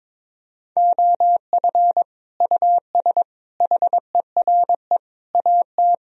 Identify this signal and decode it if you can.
Morse code, heard as a keyed tone, about 22 wpm, 700 Hz. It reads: OF US HERE AT
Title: Speed-Racing - Sets of 5 Words 22wpm